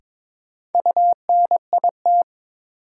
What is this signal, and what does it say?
Morse code, heard as a keyed tone, about 22 wpm, 700 Hz. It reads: UNIT
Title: Sentences from Top 1000 Words - Review Part 2 - New Word Drill - 22wpm